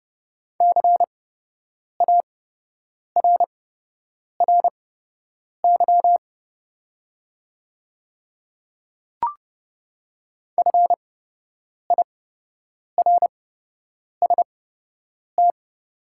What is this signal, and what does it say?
Morse code, heard as a keyed tone, about 30 wpm, spaced as for 8 wpm, 700 Hz. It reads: CARRY FIRST
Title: 8x Character Spacing - Top 300 Words - 30wpm